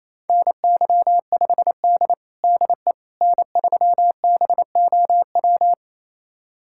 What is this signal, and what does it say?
Morse code, heard as a keyed tone, about 28 wpm, 700 Hz. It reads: NY5D DE N3BOW
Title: USA Callsign DE Callsign 28wpm